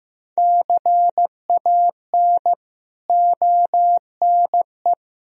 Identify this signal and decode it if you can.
Morse code, heard as a keyed tone, about 16 wpm, 700 Hz. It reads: CAN ONE
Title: Sentences from Top 100 Words and Abbreviations 15wpm